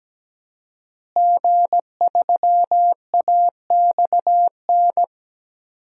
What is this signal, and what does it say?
Morse code, heard as a keyed tone, about 17 wpm, 700 Hz. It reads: G3AXN
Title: DXCC-100 Call Signs - 5 Characters - Repeated 3x - 17wpm